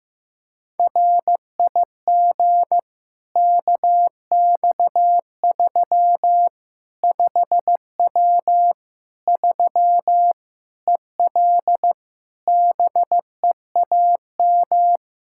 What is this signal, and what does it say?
Morse code, heard as a keyed tone, about 15 wpm, 700 Hz. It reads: RIG KX3 5W 3 EL BEAM
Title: K1USN SST QSO Practice 15wpm